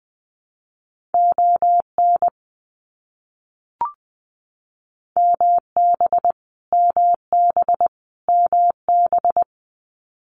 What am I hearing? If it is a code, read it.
Morse code, heard as a keyed tone, about 20 wpm, 700 Hz. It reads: ON MB MB MB
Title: Canadian Provinces and Territories - Repeated 3x - 20wpm